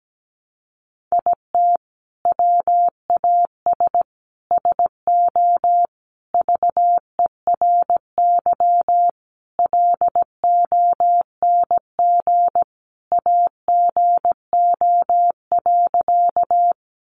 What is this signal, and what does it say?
Morse code, heard as a keyed tone, about 17 wpm, 700 Hz. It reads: IT WAS SO VERY LONG AGO.
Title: Sentences from Top 500 Words 17wpm